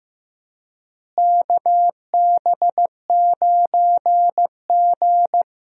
Morse code, heard as a keyed tone, about 15 wpm, 700 Hz. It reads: KB9G
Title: USA Call Signs - 4 Characters - Repeated 2x - 15wpm